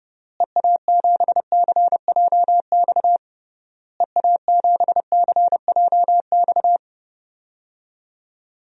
Morse code, heard as a keyed tone, about 30 wpm, 700 Hz. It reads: EA7CJX EA7CJX
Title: DXCC-100 Call Signs - 6 Characters - Repeated 3x - 30wpm